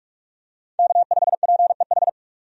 Morse code, heard as a keyed tone, about 42 wpm, 700 Hz. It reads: K5PEH
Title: USA Call Signs - 5 Characters - 45wpm